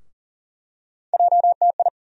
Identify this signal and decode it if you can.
Morse code, heard as a keyed tone, about 40 wpm, 700 Hz. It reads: JTI
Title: USA Call Sign Postfixes - 3 Letters 40wpm